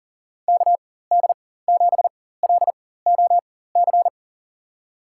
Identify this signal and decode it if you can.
Morse code, heard as a keyed tone, about 38 wpm, spaced as for 18 wpm, 700 Hz. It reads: KD7LOC
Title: USA Call Signs - 6 Characters - 4x Character Spacing - 40wpm